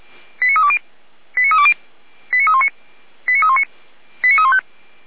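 Five-tone selective call (ZVEI), heard as quick runs of five tones, tones of about 80 ms. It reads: 89320 89320 89319 89319 89315
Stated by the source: ZVEI five-tone selective call, received by SDR.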